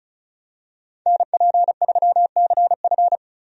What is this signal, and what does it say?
Morse code, heard as a keyed tone, about 34 wpm, 700 Hz. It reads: NP3CF
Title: USA Call Signs - 5 Characters - Repeated 2x - 35wpm